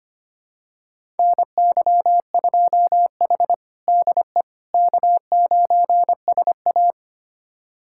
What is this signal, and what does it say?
Morse code, heard as a keyed tone, about 25 wpm, 700 Hz. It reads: NY2H DE K9SA
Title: USA Callsign DE Callsign - 4 Characters - 25wpm